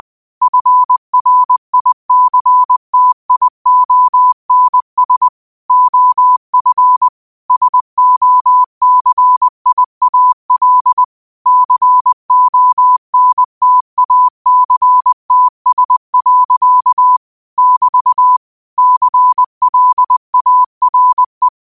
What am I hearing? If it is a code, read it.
Morse code, heard as a keyed tone, about 20 wpm, 1000 Hz. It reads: FRICTIONS OF SOCIAL CONTACTS. <BT> CLARE
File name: qotdfiles / 20wpm